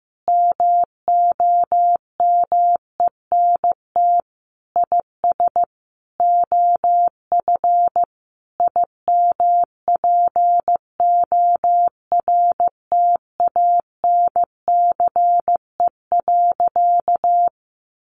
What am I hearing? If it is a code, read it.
Morse code, heard as a keyed tone, about 14 wpm, 700 Hz. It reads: MOMENT IS OF IMPORTANCE.
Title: Sentences from Top 1200 Words 15wpm